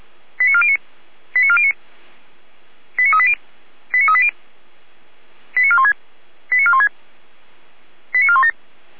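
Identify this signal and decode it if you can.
Five-tone selective call (ZVEI), heard as quick runs of five tones, tones of about 70 ms. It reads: 89409 89409 89380 89380 89416 89416 89417
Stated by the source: ZVEI five-tone selective call 89013 received on SDR.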